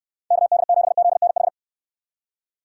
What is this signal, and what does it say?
Morse code, heard as a keyed tone, about 69 wpm, 700 Hz. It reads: KD7ZNH